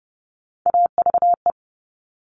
Morse code, heard as a keyed tone, about 30 wpm, 700 Hz. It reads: AVE
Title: Top Three Letter Combinations in Words 30wpm